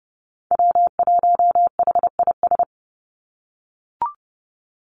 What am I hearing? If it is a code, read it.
Morse code, heard as a keyed tone, about 30 wpm, 700 Hz. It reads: W1HIS
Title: Call Signs - 30wpm